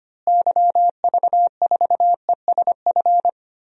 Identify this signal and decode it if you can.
Morse code, heard as a keyed tone, about 25 wpm, 700 Hz. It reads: YV4ESF
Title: DXCC-100 Call Signs - 6 Characters - Repeated 3x - 25wpm